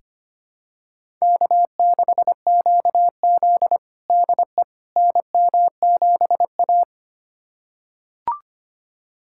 Morse code, heard as a keyed tone, about 25 wpm, 700 Hz. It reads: K6QZ DE NM7A
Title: USA Callsign DE Callsign - 4 Characters - 25wpm